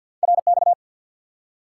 Morse code, heard as a keyed tone, about 53 wpm, 700 Hz. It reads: RX
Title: Top 90 QSO Elements 50wpm